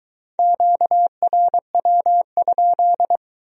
Morse code, heard as a keyed tone, about 22 wpm, 700 Hz. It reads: QRW?